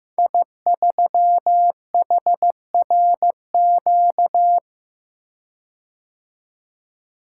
Morse code, heard as a keyed tone, about 15 wpm, 700 Hz. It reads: I3HRQ
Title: Code Groups - 5 Letter:Numbers - 15wpm